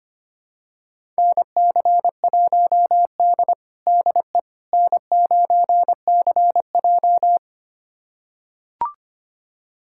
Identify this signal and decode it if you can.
Morse code, heard as a keyed tone, about 25 wpm, 700 Hz. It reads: NC1D DE N9CJ
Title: USA Callsign DE Callsign - 4 Characters - 25wpm